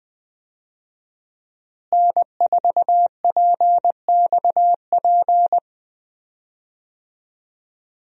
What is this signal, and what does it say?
Morse code, heard as a keyed tone, about 20 wpm, 700 Hz. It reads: N4PXP
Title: USA Call Signs - 5 Characters - Repeated 3x - 20wpm